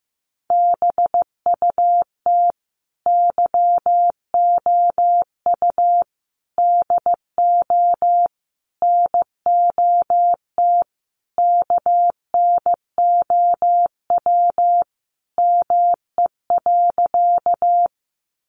Morse code, heard as a keyed tone, about 15 wpm, 700 Hz. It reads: BUT YOU DO NOT KNOW ME.
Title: Sentences from Top 600 Words 15wpm